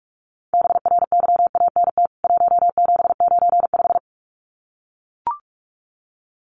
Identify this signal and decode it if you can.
Morse code, heard as a keyed tone, about 44 wpm, 700 Hz. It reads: BRYANT 1795
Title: CWT Exchange 45wpm